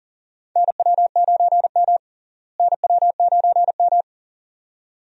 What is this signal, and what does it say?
Morse code, heard as a keyed tone, about 40 wpm, 700 Hz. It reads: NW9M NW9M
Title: USA Call Signs - 4 Characters - Repeated 2x - 40wpm